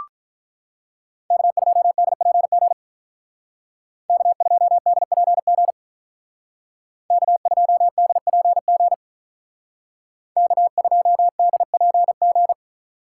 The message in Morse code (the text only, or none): K2DPG K2DPG K2DPG K2DPG